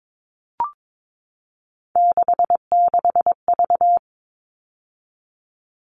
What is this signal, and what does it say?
Morse code, heard as a keyed tone, about 22 wpm, 700 Hz. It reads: 66V
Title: All Cut Numbers - 3-Digit 22wpm